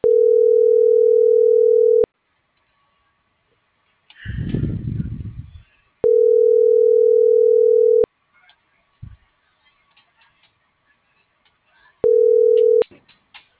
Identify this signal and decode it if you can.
Ringback tone (North America), 440 + 480 Hz.